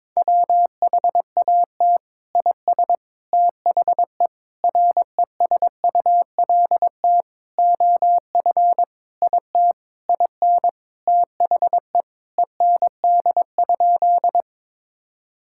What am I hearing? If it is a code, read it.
Morse code, heard as a keyed tone, about 22 wpm, 700 Hz. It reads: WHAT IS THE RESULT OF IT IN THE END?
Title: Sentences from Top 1000 Words - Review Part 5 - 22wpm